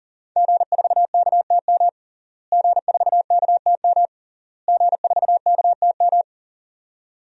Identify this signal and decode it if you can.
Morse code, heard as a keyed tone, about 37 wpm, 700 Hz. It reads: G4KTM G4KTM G4KTM
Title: DXCC-100 Call Signs - 5 Characters - Repeated 3x - 40wpm